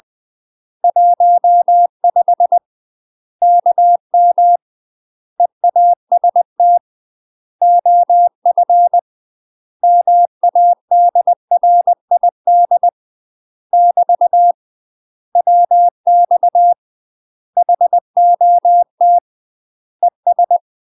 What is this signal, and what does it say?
Morse code, heard as a keyed tone, about 20 wpm, 700 Hz. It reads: 15 KM EAST OF MADRID <BT> WX HOT ES
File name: UK MASTER QSOs TRAINING / 20WPM / UK MASTER QSOs 0041